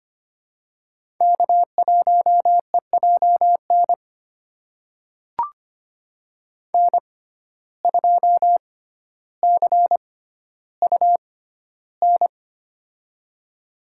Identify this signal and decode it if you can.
Morse code, heard as a keyed tone, about 25 wpm, spaced as for 13 wpm, 700 Hz. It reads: K1EJN N2CUN
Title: USA Call Signs - 5 Characters - 6x Character Spacing - 25wpm